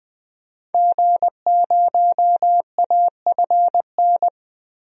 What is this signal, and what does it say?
Morse code, heard as a keyed tone, about 20 wpm, 700 Hz. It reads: G0AFN
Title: DXCC-100 Call Signs - 5 Characters - 20wpm